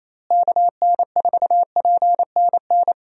Morse code, heard as a keyed tone, about 28 wpm, 700 Hz. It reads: KN4PNN
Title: USA Call Signs - 6 Characters - Repeated 3x - 28wpm